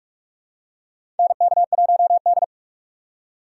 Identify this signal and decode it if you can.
Morse code, heard as a keyed tone, about 43 wpm, 700 Hz. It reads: NK1D
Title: USA Call Signs - 4 Characters - 3x Character Spacing - 45wpm